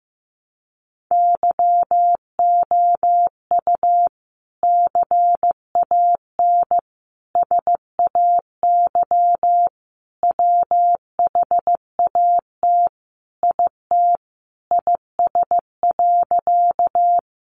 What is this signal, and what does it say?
Morse code, heard as a keyed tone, about 15 wpm, 700 Hz. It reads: YOU CAN SAY WHAT IT IS.
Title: Sentences from Top 200 Words 15wpm